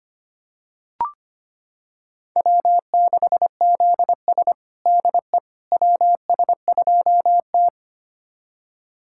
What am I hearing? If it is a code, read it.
Morse code, heard as a keyed tone, about 25 wpm, 700 Hz. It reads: W6ZS DE WS2T
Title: USA Callsign DE Callsign - 4 Characters - 25wpm